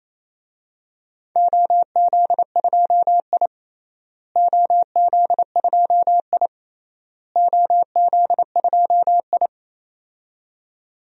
Morse code, heard as a keyed tone, about 27 wpm, 700 Hz. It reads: OZ2I OZ2I OZ2I
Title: DXCC-100 Call Signs - 4 Characters - Repeated 3x - 28wpm